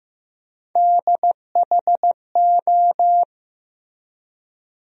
Morse code, heard as a keyed tone, about 15 wpm, 700 Hz. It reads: DHO
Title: USA Call Sign Postfixes - 3 Letters 15wpm